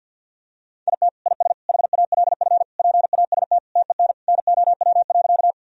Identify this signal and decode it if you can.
Morse code, heard as a keyed tone, about 50 wpm, 700 Hz. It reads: IT IS HALF PAST TEN NOW.
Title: Sentences from Top 1000 Words - Review Part 3 - 50wpm